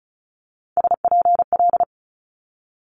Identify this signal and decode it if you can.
Morse code, heard as a keyed tone, about 36 wpm, 700 Hz. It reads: SPL
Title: Multi-letter Phonemes Quick Fire 35wpm